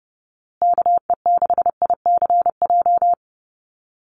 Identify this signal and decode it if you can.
Morse code, heard as a keyed tone, about 30 wpm, 700 Hz. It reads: KE6ICJ
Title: USA Call Signs Repeated 2x 30wpm